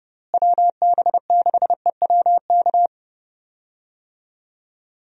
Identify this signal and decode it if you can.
Morse code, heard as a keyed tone, about 32 wpm, 700 Hz. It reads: WB6EWK
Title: USA Call Signs - 6 Characters - Repeated 3x - 30wpm